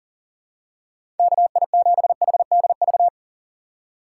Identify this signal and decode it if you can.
Morse code, heard as a keyed tone, about 40 wpm, 700 Hz. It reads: KI7HDV
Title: USA Call Signs - 2x Character Spacing - 40wpm